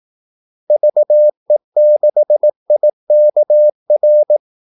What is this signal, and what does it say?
Morse code, heard as a keyed tone, about 18 wpm, 600 Hz. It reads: VE6IKR
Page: Kaldesignaler 18 wpm